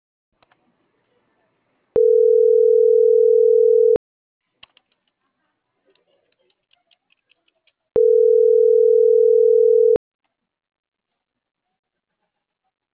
Ringback tone (North America), 440 + 480 Hz.